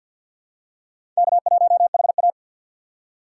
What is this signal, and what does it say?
Morse code, heard as a keyed tone, about 53 wpm, 700 Hz. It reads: K1HA
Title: USA Call Signs - 4 Characters - 2x Character Spacing - 50wpm